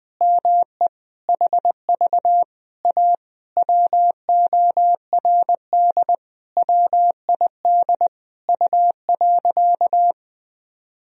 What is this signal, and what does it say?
Morse code, heard as a keyed tone, about 20 wpm, 700 Hz. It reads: ME HV A WORD WID U.
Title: Sentences from Top 300 Words and Abbreviations 20wpm